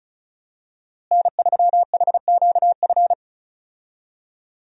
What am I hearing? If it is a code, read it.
Morse code, heard as a keyed tone, about 34 wpm, 700 Hz. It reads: N3HQF
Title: USA Call Signs - 5 Characters - Repeated 2x - 35wpm